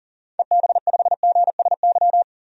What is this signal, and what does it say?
Morse code, heard as a keyed tone, about 37 wpm, 700 Hz. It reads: EB5GSY
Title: DXCC-100 Call Signs - 6 Characters - 40wpm